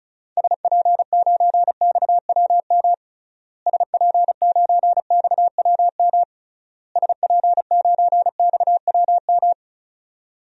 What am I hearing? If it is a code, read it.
Morse code, heard as a keyed tone, about 36 wpm, 700 Hz. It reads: SP9XWM SP9XWM SP9XWM
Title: DXCC-100 Call Signs - 6 Characters - Repeated 3x - 35wpm